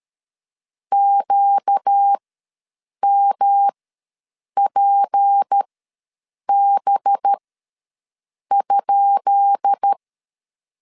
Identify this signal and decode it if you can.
Morse code, heard as a keyed tone, about 13 wpm, spaced as for 7 wpm, 800 Hz. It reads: QMPB?
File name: LEZIONI / 10 SIGNS / 10B INT ws 12wpm 3527